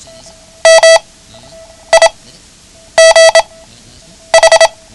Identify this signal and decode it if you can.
Morse code, heard as a keyed tone, about 27 wpm, spaced as for 8 wpm, 660 Hz. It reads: MIGH
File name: ham radio / CodeQuick / Code Quick 09